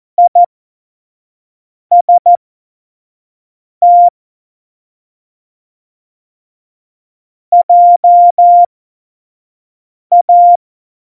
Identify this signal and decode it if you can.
Morse code, heard as a keyed tone, about 14 wpm, spaced as for 5 wpm, 700 Hz. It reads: IST JA
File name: Morsekurs / Witze / Witz 03. September 2019, 25 F70